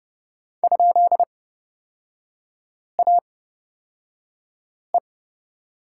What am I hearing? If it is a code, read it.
Morse code, heard as a keyed tone, about 29 wpm, spaced as for 5 wpm, 700 Hz. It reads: ?AE
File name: Lesson 38e - Letter q - Three Characters 30wpm